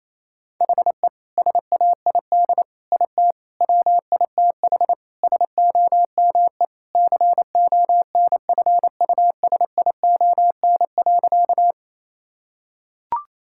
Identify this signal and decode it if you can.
Morse code, heard as a keyed tone, about 28 wpm, 700 Hz. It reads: HE SAID IT WITH SOME CONFUSION.
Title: Sentences from Top 1600 Words 28wpm